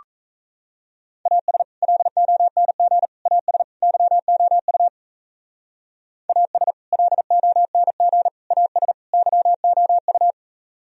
Morse code, heard as a keyed tone, about 36 wpm, 700 Hz. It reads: AS LONG AS YOU AS LONG AS YOU